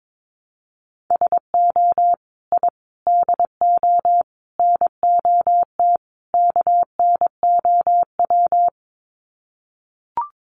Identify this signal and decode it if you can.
Morse code, heard as a keyed tone, about 22 wpm, 700 Hz. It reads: SO I DO NOT KNOW
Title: Sets of 5 Words Encore 22wpm